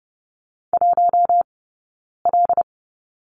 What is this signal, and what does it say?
Morse code, heard as a keyed tone, about 30 wpm, spaced as for 9 wpm, 690 Hz. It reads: 1L
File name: Lesson 17d - Letter u - Two Characters 30wpm